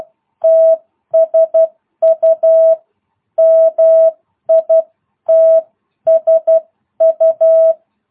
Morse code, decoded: TSU MITSU

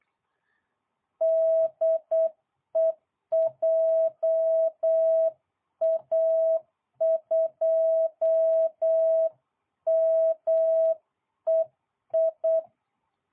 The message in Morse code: DE JA2MEI

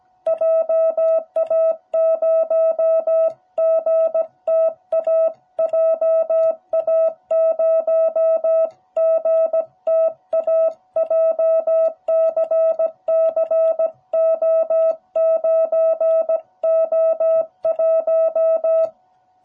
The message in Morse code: JA0GTA JA0GTA JCCO9O1

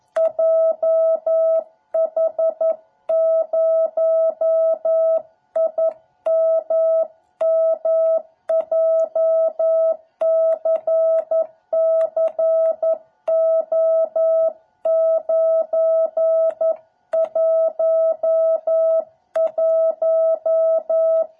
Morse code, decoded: JH0IMM JCCO911

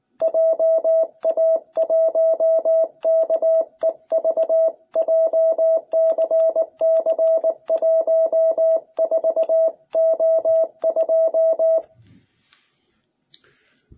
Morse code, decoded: JA1KEV JCC14O2

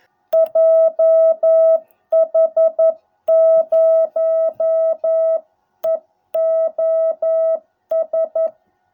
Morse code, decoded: JH0EOS

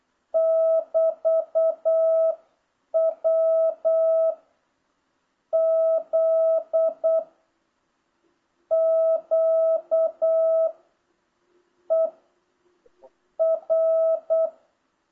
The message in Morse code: <BT> W Z Q E R

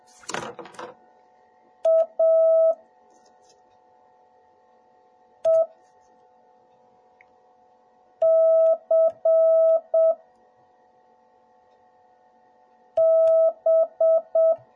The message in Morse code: AECB

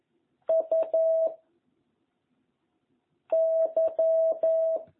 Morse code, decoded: U Y

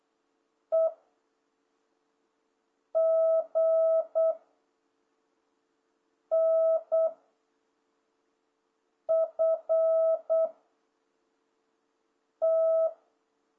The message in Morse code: E G N F T

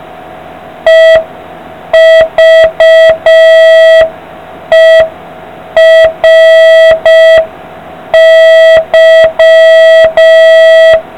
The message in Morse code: EVERY

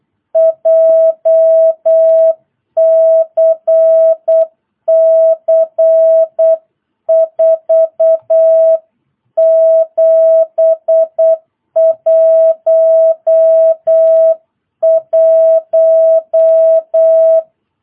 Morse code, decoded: JCC 4711

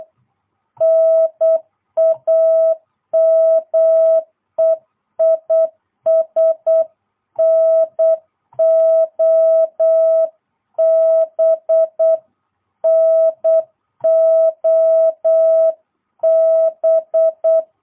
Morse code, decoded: NAME IS NOB NOB

